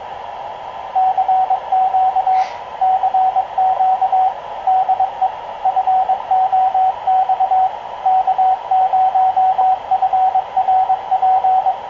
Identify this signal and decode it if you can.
Morse code, heard as a keyed tone, about 22 wpm, 750 Hz, with radio noise. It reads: CQ CQ DE FOX K0FRP
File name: k0frp ki0ii wa8bxn fox utc